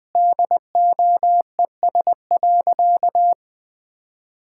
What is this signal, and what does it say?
Morse code, heard as a keyed tone, about 20 wpm, 700 Hz. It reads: DOES.
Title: Sentences with 5-Word Phrases - 20wpm